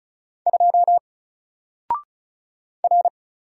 Morse code, heard as a keyed tone, about 36 wpm, 700 Hz. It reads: 2 R
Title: Single Character (Letter:Number:Punctuation:Prosign) Alt 35wpm